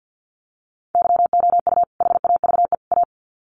Morse code, heard as a keyed tone, about 50 wpm, 700 Hz. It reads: YOU HAVE A
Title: Sets of 3 Words 50wpm